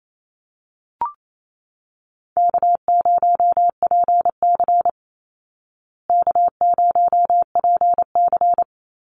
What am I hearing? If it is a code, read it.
Morse code, heard as a keyed tone, about 28 wpm, 700 Hz. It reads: K0PC K0PC
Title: Easy Call Signs Repeated 2x - 28wpm